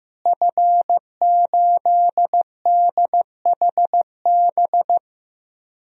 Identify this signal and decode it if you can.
Morse code, heard as a keyed tone, about 16 wpm, 700 Hz. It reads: F8DHB
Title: DXCC-100 Call Signs - 5 Characters - Repeated 3x - 15wpm